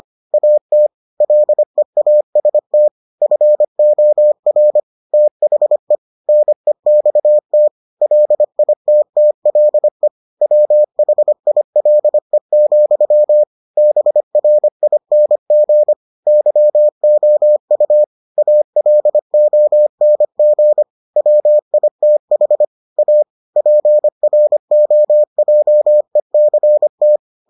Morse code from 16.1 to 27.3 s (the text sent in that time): YOU ALONG WITH A PROJECT